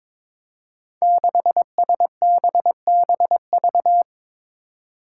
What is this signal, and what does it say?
Morse code, heard as a keyed tone, about 22 wpm, 700 Hz. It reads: 6SBBV
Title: Five Challenging Letter-Number Combinations 22wpm